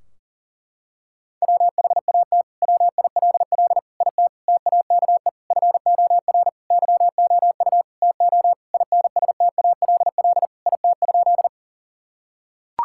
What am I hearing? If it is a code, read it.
Morse code, heard as a keyed tone, about 40 wpm, 700 Hz. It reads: WHAT WILL IT TAKE FOR YOU TO INSTALL IT?
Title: Sentences from Top 1600 Words 40wpm